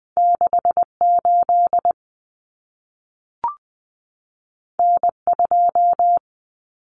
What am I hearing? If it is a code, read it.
Morse code, heard as a keyed tone, about 20 wpm, 700 Hz. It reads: 68 N2
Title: Common Cut Numbers - 2-Digit 20wpm